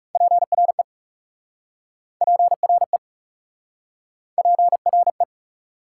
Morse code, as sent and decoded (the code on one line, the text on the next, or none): .--. .-. / .--. .-. / .--. .-. .
PRE PRE PRE